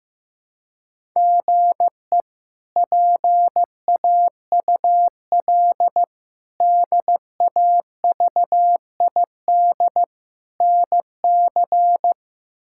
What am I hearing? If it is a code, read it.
Morse code, heard as a keyed tone, about 16 wpm, 700 Hz. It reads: GE PAUL DAVID NC
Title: K1USN SST QSO Practice - 15wpm